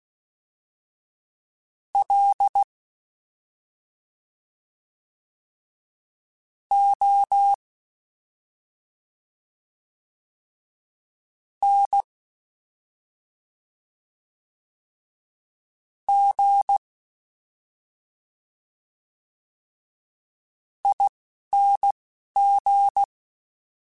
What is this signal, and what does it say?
Morse code, heard as a keyed tone, about 16 wpm, spaced as for 3 wpm, 800 Hz. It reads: LONGING
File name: Special INGs-mp3s / Special INGs-mp3s / Slow ing @10